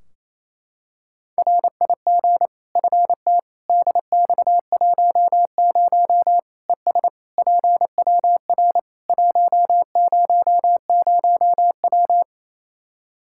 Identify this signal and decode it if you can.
Morse code, heard as a keyed tone, about 28 wpm, 700 Hz. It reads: RIG FT DX10 ES PWR 100W
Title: Practice QSO - Exchange 2 - Info - 28wpm